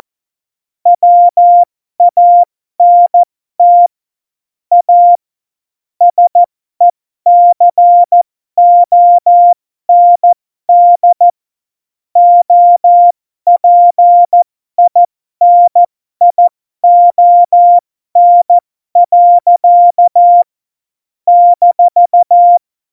Morse code, heard as a keyed tone, about 14 wpm, spaced as for 12 wpm, 700 Hz. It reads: WANT A SECOND OPINION. -
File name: Morsekurs / Witze